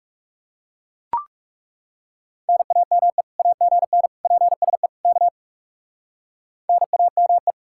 Morse code, heard as a keyed tone, about 37 wpm, 700 Hz. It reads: NAME AGN PSE K NAME
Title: Speed-Racing Common QSO Phrases - 30wpm